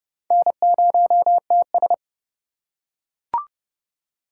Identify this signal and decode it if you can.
Morse code, heard as a keyed tone, about 29 wpm, 700 Hz. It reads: N0TS